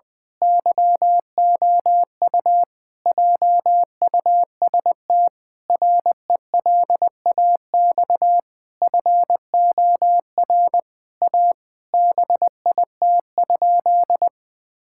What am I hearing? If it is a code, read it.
Morse code, heard as a keyed tone, about 20 wpm, 700 Hz. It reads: YOU JUST RELAX FOR A BIT?
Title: Sentences with 5-Word Phrases - 20wpm